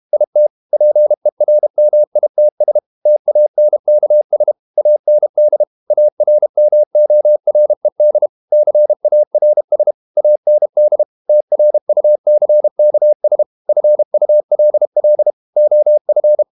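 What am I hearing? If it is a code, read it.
Morse code, heard as a keyed tone, about 32 wpm, 600 Hz. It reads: IT PERMITS TANKS AND ARMORED CARS AND TRUCKS FULL OF
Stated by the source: Morse code practice at 32 wpm from CW med Gnister, reading Ken Follett's novel Never in English.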